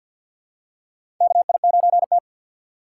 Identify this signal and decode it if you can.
Morse code, heard as a keyed tone, about 50 wpm, 700 Hz. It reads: KI9T
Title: USA Call Signs - 2x Character Spacing - 50wpm